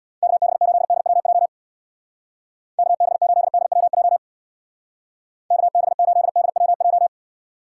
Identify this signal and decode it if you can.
Morse code, heard as a keyed tone, about 60 wpm, 700 Hz. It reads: KB8DWJ KB8DWJ KB8DWJ
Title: Speed Racing - USA 2x3 Call Signs 50wpm